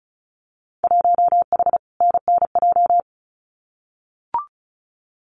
Morse code, heard as a keyed tone, about 34 wpm, 700 Hz. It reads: 1H NNJ
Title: Winter Field Day Exchanges 35wpm